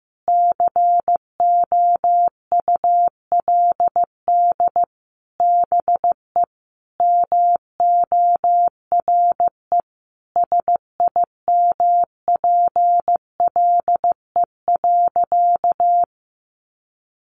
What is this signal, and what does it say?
Morse code, heard as a keyed tone, about 15 wpm, 700 Hz. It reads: COULD BE MORE SIMPLE.
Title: Sentences from Top 500 Words 15wpm